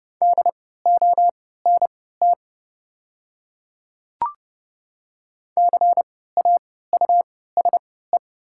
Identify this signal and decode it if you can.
Morse code, heard as a keyed tone, about 30 wpm, spaced as for 17 wpm, 690 Hz. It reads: DONT CAUSE
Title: 3x Character Spacing - Top 300 Words - 30wpm